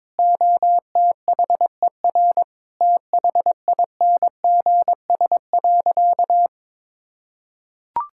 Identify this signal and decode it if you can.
Morse code, heard as a keyed tone, about 22 wpm, 700 Hz. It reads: OTHER THINGS.
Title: Sentences from Top 1000 Words - Review Part 6 - 22wpm